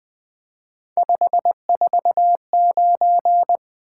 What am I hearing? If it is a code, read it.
Morse code, heard as a keyed tone, about 20 wpm, 700 Hz. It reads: 549
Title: Three Letter-Number Combinations - 20wpm